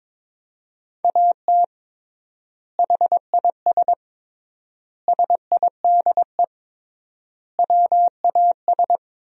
Morse code, heard as a keyed tone, about 22 wpm, 700 Hz. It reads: AT HIS SIDE WAS
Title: 3x Word-Spacing - Sentences from Top 500 Words - 22wpm